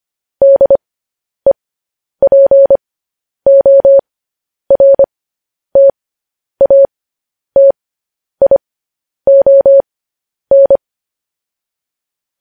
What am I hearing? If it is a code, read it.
Morse code, heard as a keyed tone, about 25 wpm, spaced as for 10 wpm, 550 Hz. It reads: DEPORTATION
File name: podcast / News headlines-1x-25x10